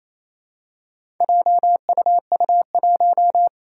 Morse code, heard as a keyed tone, about 27 wpm, 700 Hz. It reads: JUU1
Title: Four Challenging Letter-Number Combinations 28wpm